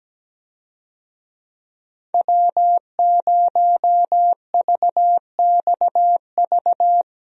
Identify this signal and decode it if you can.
Morse code, heard as a keyed tone, about 17 wpm, 700 Hz. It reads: W0VXV
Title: USA Call Signs - 5 Characters - 17wpm